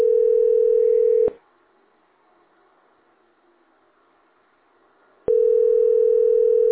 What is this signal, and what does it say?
Ringback tone (North America), 440 + 480 Hz.
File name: 9497e41a-a203-413f-97ba